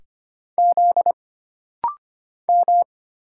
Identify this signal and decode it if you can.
Morse code, heard as a keyed tone, about 25 wpm, 700 Hz. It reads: Z M
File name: Single Character (Letter:Number:Punctuation:Prosign) - Rapid-Fire - 25wpm